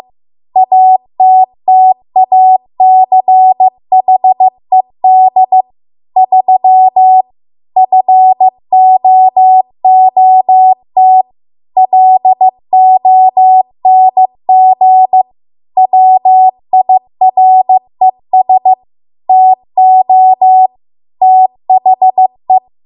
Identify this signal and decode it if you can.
Morse code, heard as a keyed tone, about 15 wpm, 750 Hz. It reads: ATTACHED 3 FOOT LONG WIRES TO THE